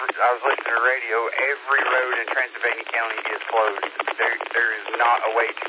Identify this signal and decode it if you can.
Emergency Alert System (SAME) tones, heard as harsh buzzing bursts, with speech. It is heard from 0.6 to 4.8 s.